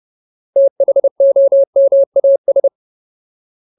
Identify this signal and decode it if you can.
Morse code, heard as a keyed tone, about 30 wpm, 550 Hz. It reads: THOMAS
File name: Ultimate Word List / 30WPM Ultimate